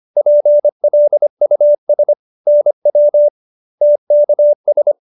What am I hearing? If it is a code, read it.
Morse code, heard as a keyed tone, about 25 wpm, 590 Hz. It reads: PLUS NW TKS